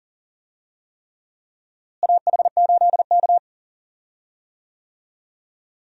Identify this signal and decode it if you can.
Morse code, heard as a keyed tone, about 40 wpm, 700 Hz. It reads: AH8K
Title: USA Call Signs - 4 Characters - Repeated 3x - 40wpm